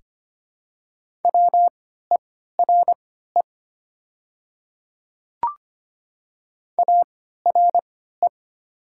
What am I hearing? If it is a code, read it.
Morse code, heard as a keyed tone, about 25 wpm, spaced as for 14 wpm, 700 Hz. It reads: WERE ARE
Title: 3x Character Spacing - Top 100 Words - 25wpm